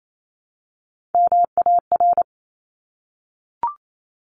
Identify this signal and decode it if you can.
Morse code, heard as a keyed tone, about 28 wpm, 700 Hz. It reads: MAR
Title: ARRL Sections 28wpm